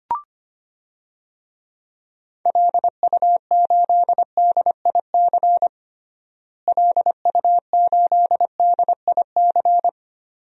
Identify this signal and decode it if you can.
Morse code, heard as a keyed tone, about 25 wpm, 700 Hz. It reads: LU8DIC LU8DIC
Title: DXCC-100 Call Signs - 6 Characters - Repeated 2x - 25wpm